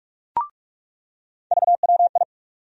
Morse code, heard as a keyed tone, about 44 wpm, 700 Hz. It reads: VWI
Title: USA Call Sign Postfixes - 3 Letters 45wpm